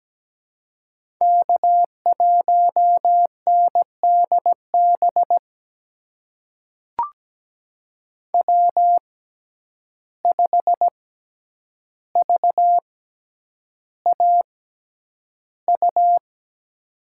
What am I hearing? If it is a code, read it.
Morse code, heard as a keyed tone, about 17 wpm, spaced as for 9 wpm, 700 Hz. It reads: K1NDB W5VAU
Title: USA Call Signs - 6x Character Spacing - 17wpm